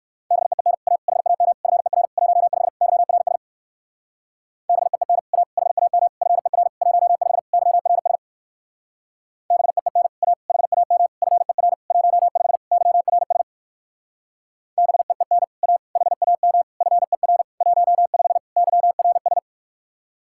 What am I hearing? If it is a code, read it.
Morse code, heard as a keyed tone, about 65 wpm, 700 Hz. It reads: BEEN A HAM FER 15 YRS BEEN A HAM FER 15 YRS BEEN A HAM FER 15 YRS BEEN A HAM FER 15 YRS